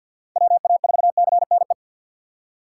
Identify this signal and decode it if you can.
Morse code, heard as a keyed tone, about 48 wpm, 700 Hz. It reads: WA4CNE
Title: USA Call Signs - 6 Characters - 6x Character Spacing - 50wpm